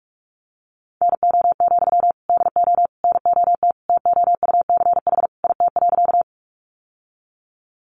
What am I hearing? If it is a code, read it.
Morse code, heard as a keyed tone, about 44 wpm, 700 Hz. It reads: NO, DO NOT TOUCH IT.
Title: Sentences from Top 1200 Words 45wpm